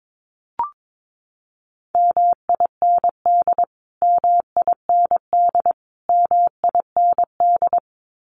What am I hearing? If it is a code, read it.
Morse code, heard as a keyed tone, about 22 wpm, 700 Hz. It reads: MIND MIND MIND
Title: Top 500 Words Repeated 22wpm